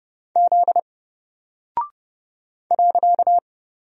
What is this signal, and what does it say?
Morse code, heard as a keyed tone, about 30 wpm, 700 Hz. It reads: Z .
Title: Single Character (Letter:Number:Punctuation:Prosign) 30wpm